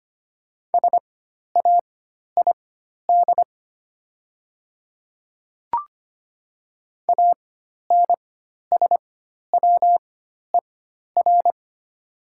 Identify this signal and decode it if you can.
Morse code, heard as a keyed tone, about 25 wpm, spaced as for 12 wpm, 700 Hz. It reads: SAID ANSWER
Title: 4x Character Spacing - Top 500 Words - 25wpm